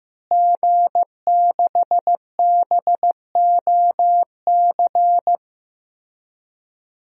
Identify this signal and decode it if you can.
Morse code, heard as a keyed tone, about 15 wpm, 700 Hz. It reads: G6BOC
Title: DXCC-100 Call Signs - 5 Characters - Repeated 3x - 15wpm